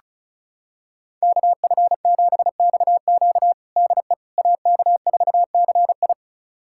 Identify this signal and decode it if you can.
Morse code, heard as a keyed tone, about 36 wpm, 700 Hz. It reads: KF7XQ DE AK4CI
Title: USA Callsign DE Callsign - 5 Characters - 35wpm